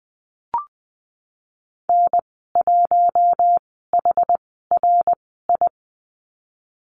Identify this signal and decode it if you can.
Morse code, heard as a keyed tone, about 20 wpm, 700 Hz. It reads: N 1 H R I
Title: 062 Lesson 11g - Letter h - Call Signs 20wpm